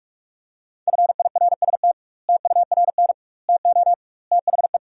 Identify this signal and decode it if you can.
Morse code, heard as a keyed tone, about 48 wpm, 700 Hz. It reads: FIRST TURN TO THE
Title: 2x Word-Spacing - Sentences from Top 300 Words - 45wpm